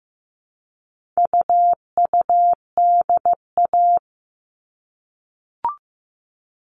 Morse code, heard as a keyed tone, about 15 wpm, 700 Hz. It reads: UUDA
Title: All Cut Numbers - 4-Digit 15wpm